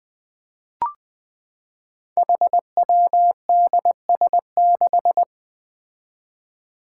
Morse code, heard as a keyed tone, about 20 wpm, 700 Hz. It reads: HWDS6
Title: Five Challenging Letter-Number Combinations 20wpm